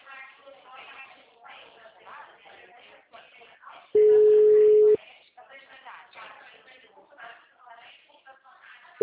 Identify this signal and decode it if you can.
Ringback tone (Europe, 425 Hz), with speech.
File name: out- -2005